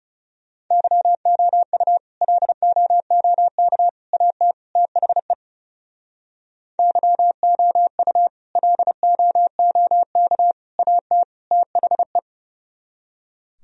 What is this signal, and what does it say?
Morse code, heard as a keyed tone, about 40 wpm, 700 Hz. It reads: YOU LOOK AT THE YOU LOOK AT THE